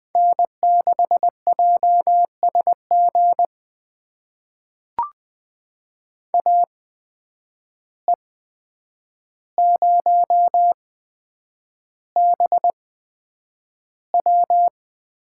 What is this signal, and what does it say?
Morse code, heard as a keyed tone, about 20 wpm, spaced as for 9 wpm, 700 Hz. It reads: N6JSG AE0BW
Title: USA Call Signs - 5 Characters - 8x Character Spacing - 20wpm